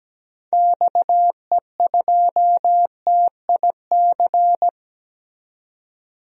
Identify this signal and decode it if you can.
Morse code, heard as a keyed tone, about 17 wpm, 700 Hz. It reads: XE2TIC
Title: DXCC-100 Call Signs - 6 Characters - 17wpm